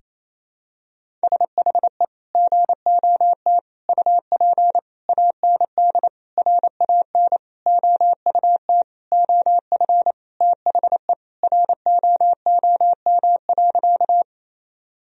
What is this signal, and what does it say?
Morse code, heard as a keyed tone, about 28 wpm, 700 Hz. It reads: SHE GOT UP AND RAN OUT OF THE ROOM.
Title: Sentences from Top 1000 Words - Review Part 3 - 28wpm